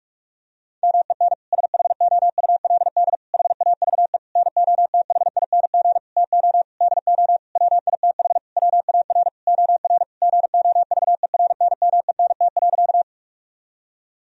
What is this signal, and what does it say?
Morse code, heard as a keyed tone, about 43 wpm, 700 Hz. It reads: MEN SHOULD HAVE NOTHING TO DO WITH WAR OR GOVERNMENT.
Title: Sentences from Top 1000 Words - Review Part 3 - 45wpm